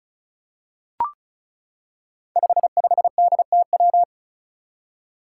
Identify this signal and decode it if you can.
Morse code, heard as a keyed tone, about 36 wpm, 700 Hz. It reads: 55DTW
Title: Five Challenging Letter-Number Combinations 35wpm